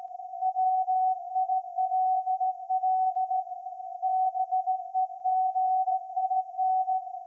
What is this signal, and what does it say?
Morse code, heard as a keyed tone, about 16 wpm, 750 Hz. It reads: WILL BEGIN
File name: iz7slz / VLF / SAQ 30-06-2024 0900